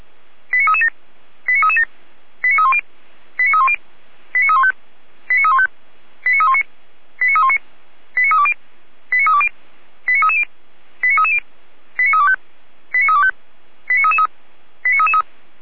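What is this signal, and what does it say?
Five-tone selective call (ZVEI), heard as quick runs of five tones, tones of about 70 ms. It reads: 89337 89337 89310 89310 89315 89315 89319 89319 89320 89320 89330 89330 89325 89325 89333
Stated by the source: ZVEI five-tone call 89304 received by SDR.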